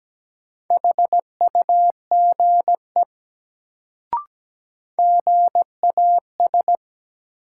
Morse code, heard as a keyed tone, about 17 wpm, 700 Hz. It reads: HUGE GAS
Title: Sentences from Top 1100 Words - New Word Drill - 17wpm